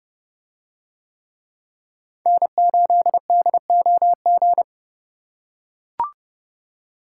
Morse code, heard as a keyed tone, about 29 wpm, 700 Hz. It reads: N8DOG